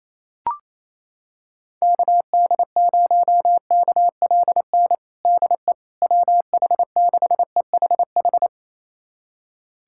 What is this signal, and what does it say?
Morse code, heard as a keyed tone, about 28 wpm, 700 Hz. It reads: KD0KLN DE WH6EHH
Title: USA Callsign DE Callsign - 6 Characters - 28wpm